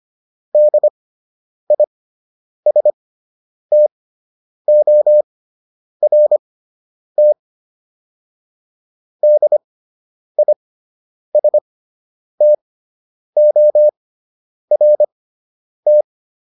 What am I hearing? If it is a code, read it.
Morse code, heard as a keyed tone, about 25 wpm, spaced as for 9 wpm, 600 Hz. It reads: DISTORT DISTORT